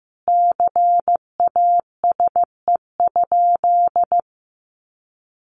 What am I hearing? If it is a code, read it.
Morse code, heard as a keyed tone, about 15 wpm, 700 Hz. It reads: CASE?
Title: Sentences from Top 1200 Words 15wpm